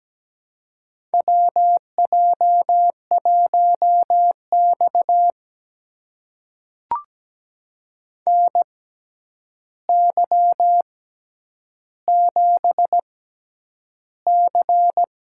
Morse code, heard as a keyed tone, about 17 wpm, spaced as for 9 wpm, 700 Hz. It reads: WJ1X NY7C
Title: USA Call Signs - 4 Characters - 6x Character Spacing - 17wpm